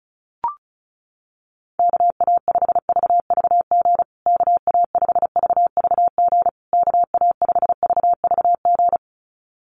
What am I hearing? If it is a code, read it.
Morse code, heard as a keyed tone, about 34 wpm, 700 Hz. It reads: KA5VVG KA5VVG KA5VVG
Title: USA Call Signs Repeated 3x 35wpm